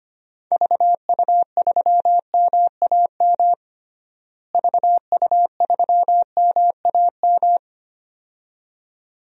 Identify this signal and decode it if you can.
Morse code, heard as a keyed tone, about 25 wpm, 700 Hz. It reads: VU3MAM VU3MAM
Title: DXCC-100 Call Signs - 6 Characters - Repeated 2x - 25wpm